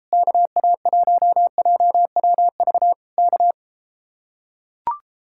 Morse code, heard as a keyed tone, about 33 wpm, 700 Hz. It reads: KA1JWV K